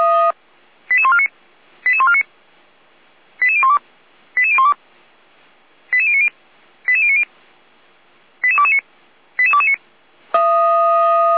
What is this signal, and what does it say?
Five-tone selective call (ZVEI), heard as quick runs of five tones, tones of about 70 ms. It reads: 88149 88149 88012 88012 88090 88090 88229 88229